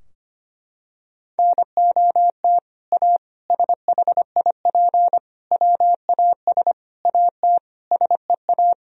Morse code, heard as a keyed tone, about 25 wpm, 700 Hz. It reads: NOT A SHIP WAS AT SEA
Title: Sentences from Top 1000 Words - Review Part 2 - 25wpm